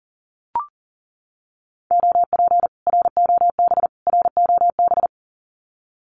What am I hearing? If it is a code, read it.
Morse code, heard as a keyed tone, about 40 wpm, 700 Hz. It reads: OP ROB ROB
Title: Common QSO Phrases Encore 40wpm